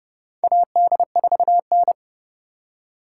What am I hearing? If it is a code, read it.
Morse code, heard as a keyed tone, about 32 wpm, 700 Hz. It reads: AD4N
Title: USA Call Signs - 4x Character Spacing - 30wpm